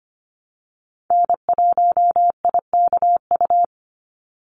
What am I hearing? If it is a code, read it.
Morse code, heard as a keyed tone, about 25 wpm, 700 Hz. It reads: N1IKU
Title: USA Call Signs 25wpm